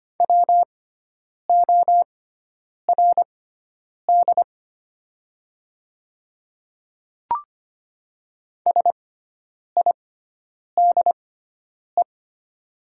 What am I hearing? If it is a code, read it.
Morse code, heard as a keyed tone, about 25 wpm, spaced as for 9 wpm, 700 Hz. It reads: WORD SIDE
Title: 6x Character Spacing - Top 100 Words - 25wpm